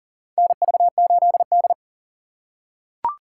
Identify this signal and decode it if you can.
Morse code, heard as a keyed tone, about 40 wpm, 700 Hz. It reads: NV8D
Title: USA Call Signs - 4x Character Spacing - 40wpm